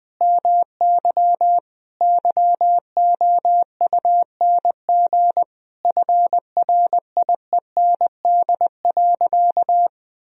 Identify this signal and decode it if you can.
Morse code, heard as a keyed tone, about 20 wpm, 700 Hz. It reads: MY YOUNG FRIEND.
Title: Sentences from Top 1000 Words - Review Part 3 - 20wpm